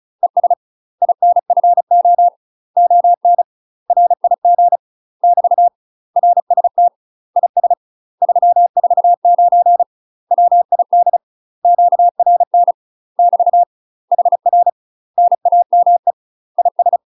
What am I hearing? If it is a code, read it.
Morse code, heard as a keyed tone, about 34 wpm, 700 Hz. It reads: ES INFO ON RIG <BT> RST IS 349 WID QRN <BT> HR NAME IS
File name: UK MASTER QSOs TRAINING / 35WPM / UK MASTER QSOs 0024